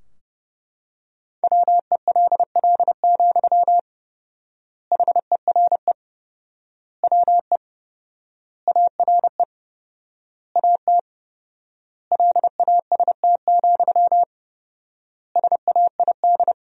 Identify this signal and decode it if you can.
Morse code, heard as a keyed tone, about 30 wpm, 700 Hz. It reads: WELL, HERE WE ARE AT LAST, SAID
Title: 4x Word-Spacing - Sentences from Top 300 Words - 30wpm